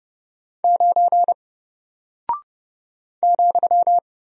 Morse code, heard as a keyed tone, about 31 wpm, 700 Hz. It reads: 9 ,
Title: Single Character (Letter:Number:Punctuation:Prosign) 30wpm